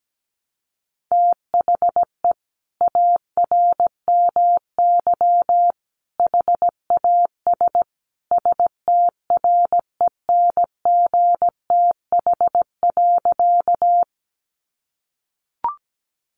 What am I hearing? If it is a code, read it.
Morse code, heard as a keyed tone, about 17 wpm, 700 Hz. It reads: THE ARMY HAS STRENGTH.
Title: Sentences from Top 1300 Words 17wpm